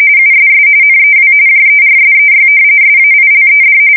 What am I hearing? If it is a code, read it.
RTTY, about 45.45 baud. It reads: USS NEW JERSEY SAILED
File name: fsk170 / battleshipnj